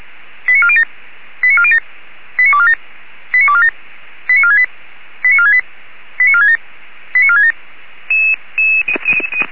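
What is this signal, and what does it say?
Five-tone selective call (ZVEI), heard as quick runs of five tones, tones of about 70 ms. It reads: 89407 89407 89247 89247 89468 89468 89467 89467
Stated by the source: ZVEI five-tone selective call received on an SDR.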